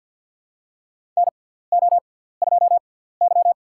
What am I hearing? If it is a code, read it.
Morse code, heard as a keyed tone, about 50 wpm, spaced as for 17 wpm, 700 Hz. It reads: NO2Y
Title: USA Call Signs - 6x Character Spacing - 50wpm